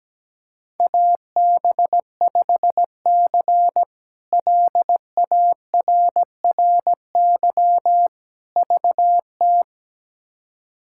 Morse code, heard as a keyed tone, about 17 wpm, 700 Hz. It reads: AB5C LARRY VT
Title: K1USN SST QSO Practice - 17wpm